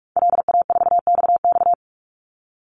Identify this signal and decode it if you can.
Morse code, heard as a keyed tone, about 44 wpm, 700 Hz. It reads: LA4XX
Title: Call Signs - 45wpm